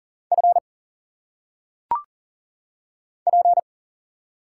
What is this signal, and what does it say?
Morse code, heard as a keyed tone, about 40 wpm, 700 Hz. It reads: F P
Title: Single Letters - Phonetic Pronunciation 40wpm